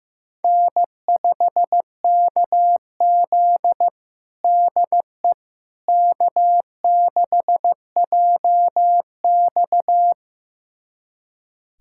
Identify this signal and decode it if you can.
Morse code, heard as a keyed tone, about 15 wpm, 700 Hz. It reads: N5KZ DE K6JX
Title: USA Callsign DE Callsign - 4 Characters - 15wpm